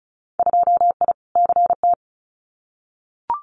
Morse code, heard as a keyed tone, about 36 wpm, 690 Hz. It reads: 2I CT
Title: Winter Field Day Exchanges 35wpm